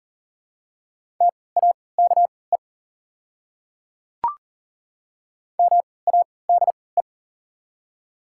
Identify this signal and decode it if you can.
Morse code, heard as a keyed tone, about 40 wpm, spaced as for 23 wpm, 700 Hz. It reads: TAKE MADE
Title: 3x Character Spacing - Top 200 Words - 40wpm